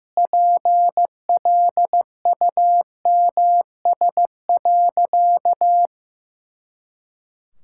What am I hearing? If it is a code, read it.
Morse code, heard as a keyed tone, about 15 wpm, 700 Hz. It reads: PLUMS.
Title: Pangram Challenge v2 - 15wpm